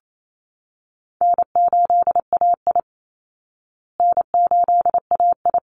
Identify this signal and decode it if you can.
Morse code, heard as a keyed tone, about 28 wpm, 700 Hz. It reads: N8AI N8AI
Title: Easy Call Signs Repeated 2x - 28wpm